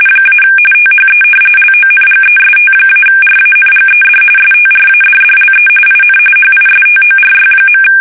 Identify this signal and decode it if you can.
RTTY, about 45.45 baud. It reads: - THE LONGEST-RANGED STRADDLE IN HISTORY.
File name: fsk850 / battleshipnj